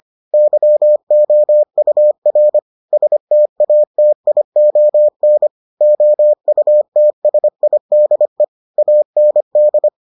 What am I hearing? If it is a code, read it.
Morse code, heard as a keyed tone, about 25 wpm, 600 Hz. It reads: YOUR STATION OUTSIDE AND